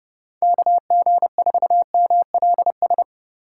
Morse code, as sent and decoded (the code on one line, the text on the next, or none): -.- --. ....- -- .-.. ...
KG4MLS